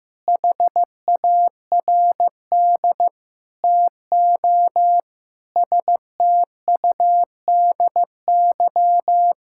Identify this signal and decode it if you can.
Morse code, heard as a keyed tone, about 15 wpm, 700 Hz. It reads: HARD TO STUDY
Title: Sentences from Top 1000 Words - Review Part 6 - 15wpm